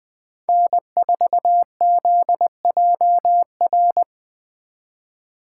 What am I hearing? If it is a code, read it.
Morse code, heard as a keyed tone, about 20 wpm, 700 Hz. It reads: N4ZJR
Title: Speed Racing - USA 1x3 Call Signs 17wpm